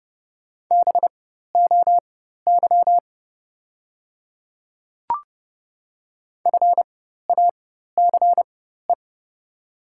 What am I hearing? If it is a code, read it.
Morse code, heard as a keyed tone, about 31 wpm, spaced as for 14 wpm, 690 Hz. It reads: BOY FACE
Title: 4x Character Spacing - Top 500 Words - 30wpm